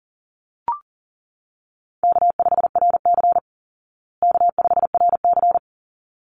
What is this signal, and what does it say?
Morse code, heard as a keyed tone, about 40 wpm, 700 Hz. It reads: K5RC K5RC
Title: Easy Call Signs Repeated 2x - 40wpm